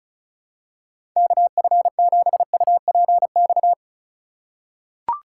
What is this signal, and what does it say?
Morse code, heard as a keyed tone, about 36 wpm, 700 Hz. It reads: KF7UPX
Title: USA Call Signs - 3x Character Spacing - 35wpm